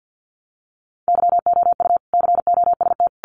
Morse code, heard as a keyed tone, about 50 wpm, 700 Hz. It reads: YOU COST
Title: Sentences from Top 800 Words 50wpm